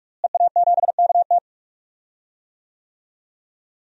Morse code, heard as a keyed tone, about 43 wpm, 700 Hz. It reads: EA7KT
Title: DXCC-100 Call Signs - 5 Characters - Repeated 3x - 45wpm